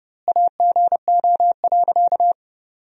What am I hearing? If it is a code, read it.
Morse code, heard as a keyed tone, about 30 wpm, 700 Hz. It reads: AGO.
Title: 2x Word-Spacing - Sentences from Top 500 Words - 30wpm